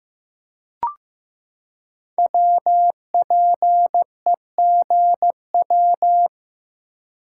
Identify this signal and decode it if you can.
Morse code, heard as a keyed tone, about 15 wpm, 700 Hz. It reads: WPEGW
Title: Code Groups - 5 Letter:Numbers - 15wpm